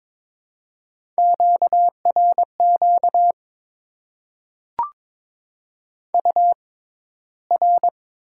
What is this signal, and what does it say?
Morse code, heard as a keyed tone, about 22 wpm, spaced as for 14 wpm, 700 Hz. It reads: QRQ UR
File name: Top 90 QSO Elements - 6x Character Spacing - 22wpm